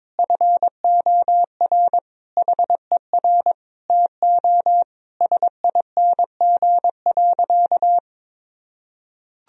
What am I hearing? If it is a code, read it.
Morse code, heard as a keyed tone, about 22 wpm, 700 Hz. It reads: FOR HER TO SING.
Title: Sentences from Top 1000 Words - Review Part 2 - 22wpm